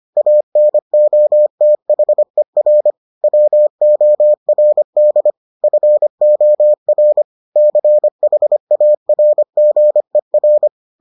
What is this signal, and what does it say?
Morse code, heard as a keyed tone, about 25 wpm, 600 Hz. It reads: ANOTHER WORD FOR CHARGER